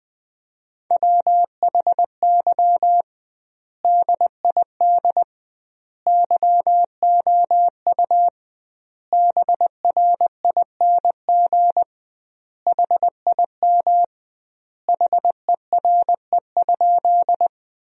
Morse code, heard as a keyed tone, about 21 wpm, 700 Hz. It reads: WHY DID YOU BRING HIM HERE?
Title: 2x Word-Spacing - Sentences from Top 500 Words - 20wpm